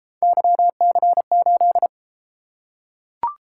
Morse code, heard as a keyed tone, about 34 wpm, 700 Hz. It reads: YC8